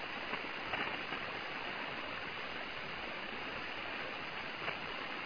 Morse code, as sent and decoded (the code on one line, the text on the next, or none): none